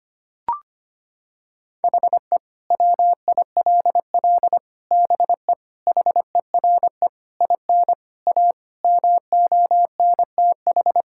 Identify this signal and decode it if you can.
Morse code, heard as a keyed tone, about 25 wpm, 700 Hz. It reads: HE WILL BE HERE IN A MONTH
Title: Sentences from Top 1000 Words - Review Part 5 - 25wpm